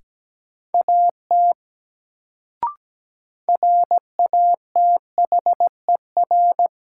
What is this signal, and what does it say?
Morse code, heard as a keyed tone, about 17 wpm, 700 Hz. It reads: AT RATHER
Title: Sets of 2 Words - New Word Drill - 17wpm